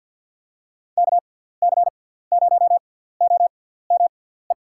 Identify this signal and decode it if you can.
Morse code, heard as a keyed tone, about 47 wpm, spaced as for 17 wpm, 700 Hz. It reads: KC0OME
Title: USA Call Signs - 6 Characters - 6x Character Spacing - 50wpm